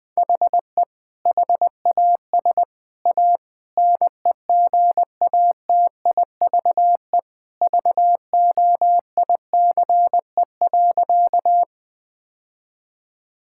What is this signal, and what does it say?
Morse code, heard as a keyed tone, about 20 wpm, 700 Hz. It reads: HE HAS A NEGATIVE VOICE.
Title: Sentences from Top 1000 Words - Review Part 6 - 20wpm